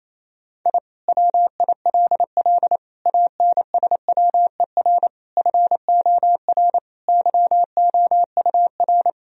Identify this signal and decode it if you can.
Morse code, heard as a keyed tone, about 28 wpm, 700 Hz. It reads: I WILL ANSWER FOR YOUR
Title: Sentences from Top 1000 Words - Review Part 4 - 28wpm